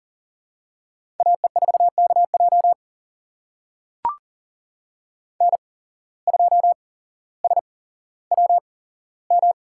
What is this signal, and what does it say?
Morse code, heard as a keyed tone, about 38 wpm, spaced as for 17 wpm, 700 Hz. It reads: AE4KJ N2SWM
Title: USA Call Signs - 5 Characters - 8x Character Spacing - 40wpm